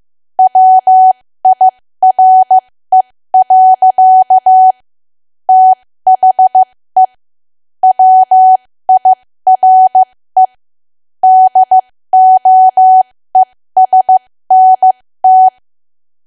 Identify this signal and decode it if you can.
Morse code, heard as a keyed tone, about 15 wpm, spaced as for 13 wpm, 750 Hz. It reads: WIRE. THE WIRE DOESNT
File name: cwa / W1AW-Level2 / 13WPM